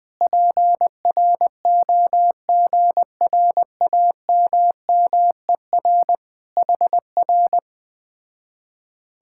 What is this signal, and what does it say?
Morse code, heard as a keyed tone, about 20 wpm, 700 Hz. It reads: PROGRAMMER HR
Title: Speed-Racing Common QSO Phrases - 17wpm